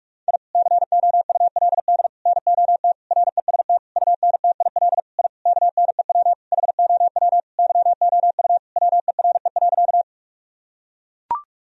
Morse code, heard as a keyed tone, about 41 wpm, 700 Hz. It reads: I COULD NOT REST UNTIL I KNEW HOW YOU WERE.
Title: Sentences from Top 1000 Words - Review Part 1 - 45wpm